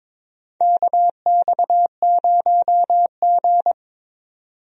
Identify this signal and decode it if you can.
Morse code, heard as a keyed tone, about 21 wpm, 700 Hz. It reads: KX0G
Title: USA Call Signs - 4 Characters - 22wpm